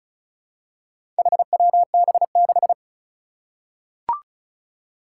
Morse code, heard as a keyed tone, about 36 wpm, 700 Hz. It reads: HWB6
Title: Four Challenging Letter-Number Combinations 35wpm